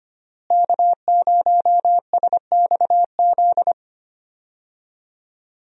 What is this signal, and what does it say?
Morse code, heard as a keyed tone, about 25 wpm, 700 Hz. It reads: K0SXZ
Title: USA Call Signs - 5 Characters - Repeated 3x - 25wpm